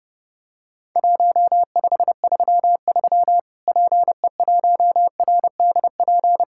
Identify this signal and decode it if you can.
Morse code, heard as a keyed tone, about 30 wpm, 700 Hz. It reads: 1533 PE1RDP
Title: Worked All Europe DX Contest - QTC Exchange - 30wpm